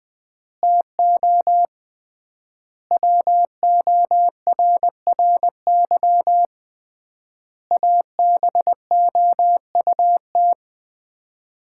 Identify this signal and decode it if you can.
Morse code, heard as a keyed tone, about 20 wpm, 700 Hz. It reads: TO WORRY ABOUT
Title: 3x Word-Spacing - Sets of 5 Words - 20wpm